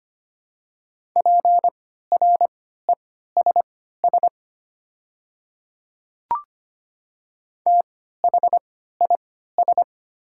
Morse code, heard as a keyed tone, about 25 wpm, spaced as for 14 wpm, 700 Hz. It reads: PRESS THIS
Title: 3x Character Spacing - Top 300 Words - 25wpm